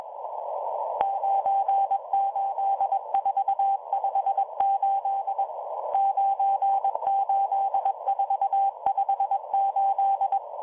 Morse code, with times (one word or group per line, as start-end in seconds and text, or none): 1.0-5.4 98458
5.9-10.4 98458